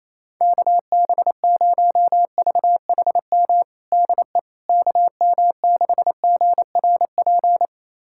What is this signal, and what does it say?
Morse code, heard as a keyed tone, about 28 wpm, 700 Hz. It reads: KB0VHM DE KM6GRP
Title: USA Callsign DE Callsign - 6 Characters - 28wpm